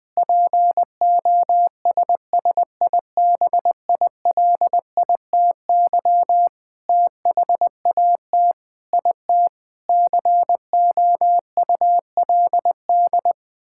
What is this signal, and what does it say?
Morse code, heard as a keyed tone, about 20 wpm, 700 Hz. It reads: POSSIBILITY THAT IT COULD
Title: Sentences with 9-Word Phrases - Spoken Beginning - 20wpm